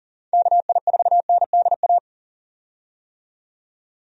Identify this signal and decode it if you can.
Morse code, heard as a keyed tone, about 40 wpm, 700 Hz. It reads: KI4NDA
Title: USA Call Signs - 6 Characters - 40wpm